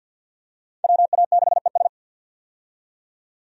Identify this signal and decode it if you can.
Morse code, heard as a keyed tone, about 50 wpm, 700 Hz. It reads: WA6ES
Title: USA Call Signs - 5 Characters - 50wpm